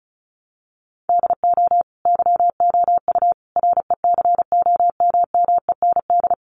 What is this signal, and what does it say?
Morse code, heard as a keyed tone, about 34 wpm, 700 Hz. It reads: DO YOU RECOMMEND
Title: Sentences from Top 1200 Words 35wpm